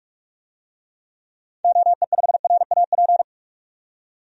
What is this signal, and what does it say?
Morse code, heard as a keyed tone, about 42 wpm, 700 Hz. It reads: OE5RAP
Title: DXCC-100 Call Signs - 6 Characters - 45wpm